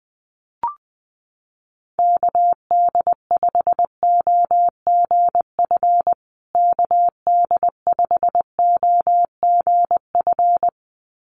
Morse code, heard as a keyed tone, about 20 wpm, 700 Hz. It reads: KD5OGF KD5OGF
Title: USA Call Signs Repeated 2x 20wpm